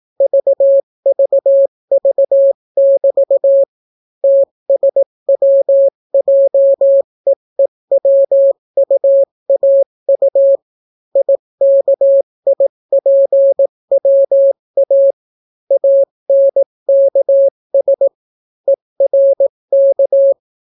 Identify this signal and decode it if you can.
Morse code, heard as a keyed tone, about 18 wpm, 550 Hz. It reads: VVV<BT> TSWJEEWUAU IKIPWA ANKS ERK